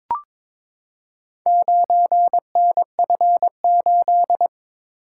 Morse code, heard as a keyed tone, about 22 wpm, 700 Hz. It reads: 9NF8
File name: practice / 22wpm / Four Letter-Number Combinations - 22wpm